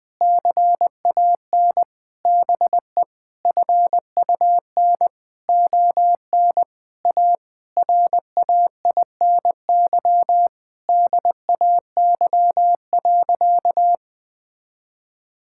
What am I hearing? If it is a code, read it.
Morse code, heard as a keyed tone, about 20 wpm, 700 Hz. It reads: CAN BE FUN ON A RAINY DAY.
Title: Difficult Sentences with Hints - 20wpm